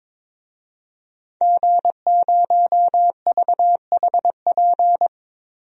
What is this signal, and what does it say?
Morse code, heard as a keyed tone, about 22 wpm, 700 Hz. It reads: G0VHP
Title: DXCC-100 Call Signs - 5 Characters - 22wpm